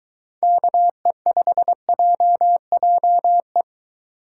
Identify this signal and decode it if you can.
Morse code, heard as a keyed tone, about 23 wpm, 700 Hz. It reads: KE5JJE